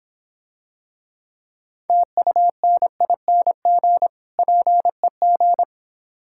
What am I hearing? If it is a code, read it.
Morse code, heard as a keyed tone, about 25 wpm, 700 Hz. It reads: TUNING PEG